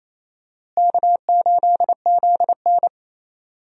Morse code, heard as a keyed tone, about 28 wpm, 700 Hz. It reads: K8ZN
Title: USA Call Signs - 4 Characters - Repeated 2x - 28wpm